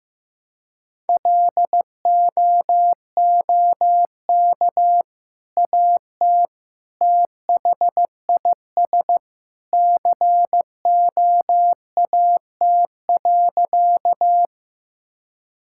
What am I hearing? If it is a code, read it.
Morse code, heard as a keyed tone, about 15 wpm, 700 Hz. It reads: LOOK AT THIS COAT.
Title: Sentences from Top 1600 Words 15wpm